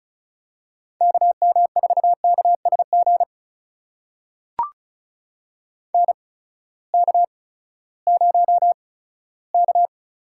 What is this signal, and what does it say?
Morse code, heard as a keyed tone, about 34 wpm, spaced as for 17 wpm, 700 Hz. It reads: KM4KSG NK0K
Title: USA Call Signs - 8x Character Spacing - 35wpm